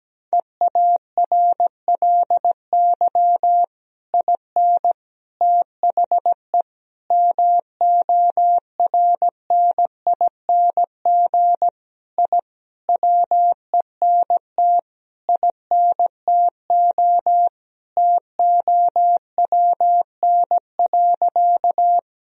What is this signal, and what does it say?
Morse code, heard as a keyed tone, about 17 wpm, 700 Hz. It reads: EARLY IN THE MORNING I WENT INTO TOWN.
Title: Sentences from Top 1000 Words - Review Part 2 - 17wpm